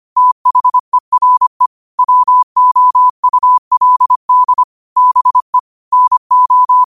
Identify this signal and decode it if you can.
Morse code, heard as a keyed tone, about 25 wpm, 1000 Hz. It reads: THERE WOULD BE NO